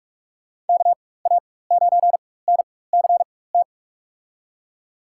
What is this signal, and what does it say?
Morse code, heard as a keyed tone, about 46 wpm, spaced as for 21 wpm, 700 Hz. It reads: KA9NCT
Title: USA Call Signs - 6 Characters - 4x Character Spacing - 45wpm